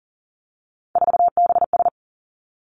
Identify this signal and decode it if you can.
Morse code, heard as a keyed tone, about 42 wpm, 700 Hz. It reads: 4BS
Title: Three Challenging Letter-Number Combinations 40wpm